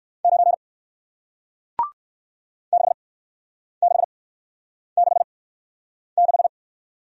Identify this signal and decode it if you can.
Morse code, heard as a keyed tone, about 56 wpm, spaced as for 9 wpm, 700 Hz. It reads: 9 6666
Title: Speed-Racing Single Letter-Number 45wpm